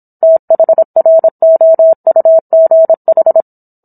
Morse code, heard as a keyed tone, about 27 wpm, 650 Hz. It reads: THROUGH